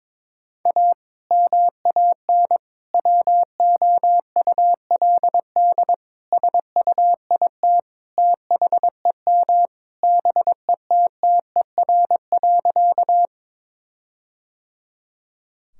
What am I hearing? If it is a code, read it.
Morse code, heard as a keyed tone, about 22 wpm, 700 Hz. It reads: A MAN WOULD SUIT THEM BETTER.
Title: Sentences from Top 1600 Words 22wpm